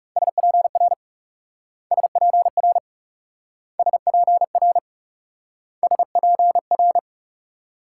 Morse code, heard as a keyed tone, about 38 wpm, 700 Hz. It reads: SPR SPR SPR SPR